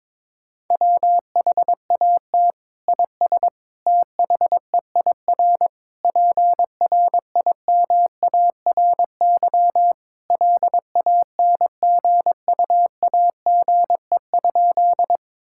Morse code, heard as a keyed tone, about 22 wpm, 700 Hz. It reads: WHAT IS THEIR PRIMARY LANGUAGE?
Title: Sentences from Top 1000 Words - Review Part 5 - 22wpm